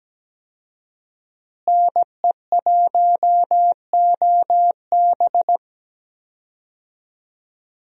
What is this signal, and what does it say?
Morse code, heard as a keyed tone, about 17 wpm, 700 Hz. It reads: NE1OB
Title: USA Call Signs - 5 Characters - Repeated 2x - 17wpm